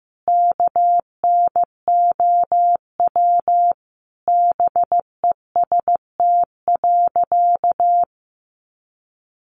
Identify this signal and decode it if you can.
Morse code, heard as a keyed tone, about 15 wpm, 700 Hz. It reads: KNOW BEST.
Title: Sentences from Top 600 Words 15wpm